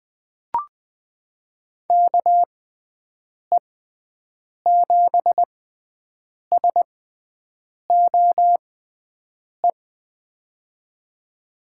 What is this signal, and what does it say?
Morse code, heard as a keyed tone, about 20 wpm, spaced as for 7 wpm, 700 Hz. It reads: KE7SOE
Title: USA Call Signs - 6 Characters - 6x Character Spacing - 20wpm